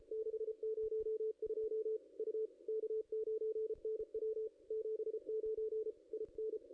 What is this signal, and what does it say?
Morse code, heard as a keyed tone, about 29 wpm, 410 Hz. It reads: 602 U K9NW 79 IN
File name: contests / ss-cw-k5tr-op-n5rz -0600